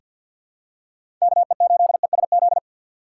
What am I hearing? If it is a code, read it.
Morse code, heard as a keyed tone, about 47 wpm, 700 Hz. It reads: KE8ESZ
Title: USA Call Signs - 6 Characters - 4x Character Spacing - 50wpm